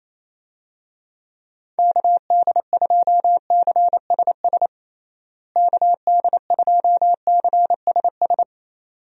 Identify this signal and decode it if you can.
Morse code, heard as a keyed tone, about 28 wpm, 700 Hz. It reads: KD2CSS KD2CSS
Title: USA Call Signs - 6 Characters - Repeated 2x - 28wpm